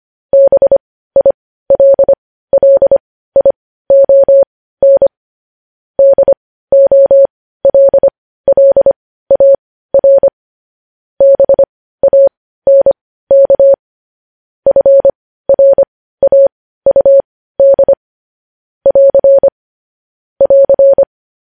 Morse code, heard as a keyed tone, about 25 wpm, spaced as for 15 wpm, 550 Hz. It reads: BILLION DOLLAR BANK FRAUD <AR> <AR>
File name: podcast / News headlines-1x-25x15